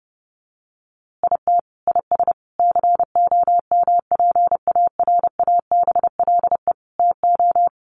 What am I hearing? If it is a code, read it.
Morse code, heard as a keyed tone, about 28 wpm, 700 Hz. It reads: IT IS COMPARABLE TO
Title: A Sound Of Thunder 30wpm